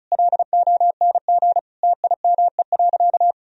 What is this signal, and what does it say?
Morse code, heard as a keyed tone, about 36 wpm, 700 Hz. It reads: LONG TIME.
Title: Sentences from Top 1000 Words - Review Part 2 - 35wpm